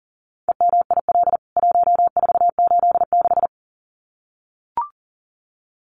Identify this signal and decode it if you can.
Morse code, heard as a keyed tone, about 37 wpm, 700 Hz. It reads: EMIL 1486
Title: CWT Exchange 40wpm